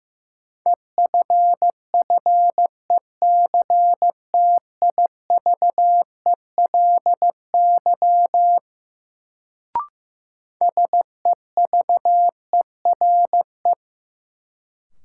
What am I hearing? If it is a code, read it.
Morse code, heard as a keyed tone, about 15 wpm, 700 Hz. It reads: EFFECTIVELY SEVERE
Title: Sentences from Top 1400 Words - New Word Drill - 15wpm